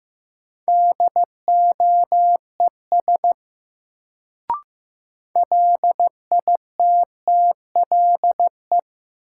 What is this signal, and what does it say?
Morse code, heard as a keyed tone, about 15 wpm, 700 Hz. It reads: DOES LITTLE
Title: Sets of 2 Words - New Word Drill - 15wpm